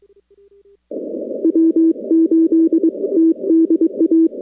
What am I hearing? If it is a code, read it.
Morse code, heard as a keyed tone, about 22 wpm, 330 Hz. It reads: W8TDA
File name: contests / 2019-cw-wpx-kt5j-k5pi- -2330 0-41